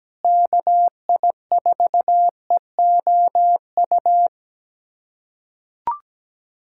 Morse code, heard as a keyed tone, about 17 wpm, 700 Hz. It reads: KI4EOU
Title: USA Callsign DE Callsign - 6 Characters - 17wpm